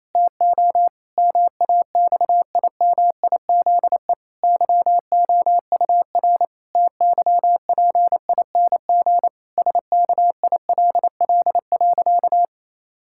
Morse code, heard as a keyed tone, about 28 wpm, 700 Hz. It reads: TO MAXIMIZE YOUR TYPING SKILL.
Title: Pangram Challenge v2 - 28wpm